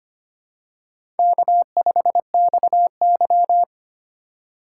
Morse code, heard as a keyed tone, about 25 wpm, 700 Hz. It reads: K5XY
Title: USA Call Signs - 4 Characters - 4x Character Spacing - 25wpm